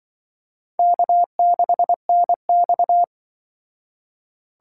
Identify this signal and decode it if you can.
Morse code, heard as a keyed tone, about 25 wpm, 700 Hz. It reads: K6NX
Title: Speed Racing - USA 1x2 Call Signs 20wpm